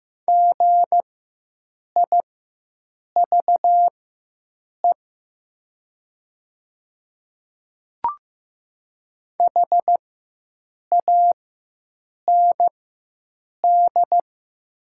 Morse code, heard as a keyed tone, about 15 wpm, spaced as for 7 wpm, 700 Hz. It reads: GIVE HAND
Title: 4x Character Spacing - Top 200 Words - 15wpm